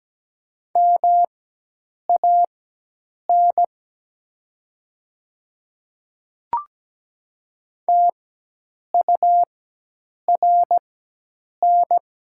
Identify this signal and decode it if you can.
Morse code, heard as a keyed tone, about 17 wpm, spaced as for 8 wpm, 700 Hz. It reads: MAN TURN
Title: 4x Character Spacing - Top 200 Words - 17wpm